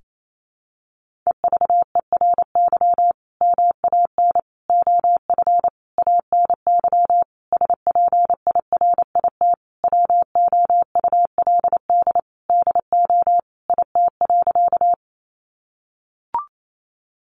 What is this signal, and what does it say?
Morse code, heard as a keyed tone, about 28 wpm, 700 Hz. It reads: EVERY MAN OF ANY SPIRIT WOULD DO IT.
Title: Sentences from Top 1400 Words 28wpm